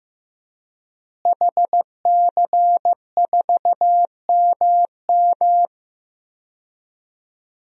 Morse code, heard as a keyed tone, about 15 wpm, 700 Hz. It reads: HC4MM
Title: DXCC-100 Call Signs - 5 Characters - Repeated 2x - 15wpm